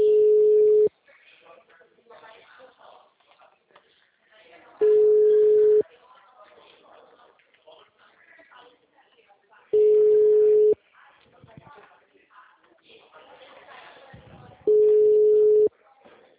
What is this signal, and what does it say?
Ringback tone (Europe, 425 Hz), with speech.
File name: out- -2011- - - .18151